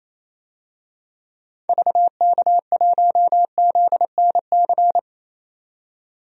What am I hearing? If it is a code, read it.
Morse code, heard as a keyed tone, about 28 wpm, 700 Hz. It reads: VK1ZNC
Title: DXCC-100 Call Signs - 6 Characters - 28wpm